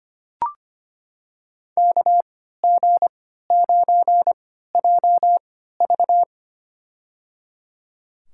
Morse code, heard as a keyed tone, about 25 wpm, spaced as for 14 wpm, 700 Hz. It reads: KG9JV
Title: USA Call Signs - 5 Characters - 3x Character Spacing - 25wpm